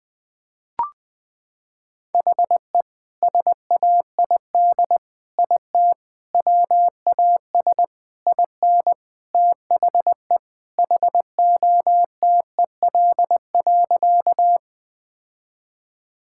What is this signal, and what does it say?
Morse code, heard as a keyed tone, about 20 wpm, 700 Hz. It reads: HE SAID IT WAS IN THE HOTEL.
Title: Sentences from Top 1600 Words 20wpm